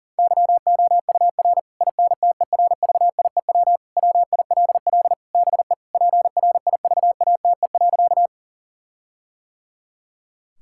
Morse code, heard as a keyed tone, about 40 wpm, 700 Hz. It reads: YOUR INTERVIEW WILL BE PRIVATE.
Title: Sentences from Top 1600 Words 40wpm